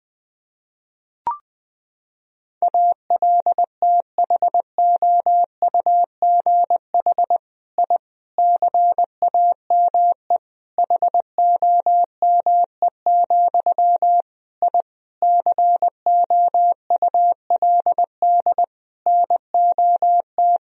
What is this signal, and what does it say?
Morse code, heard as a keyed tone, about 21 wpm, 700 Hz. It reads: ALTHOUGH I CAME HOME, I COULD NOT
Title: Sentences from Top 1000 Words - Review Part 3 - 20wpm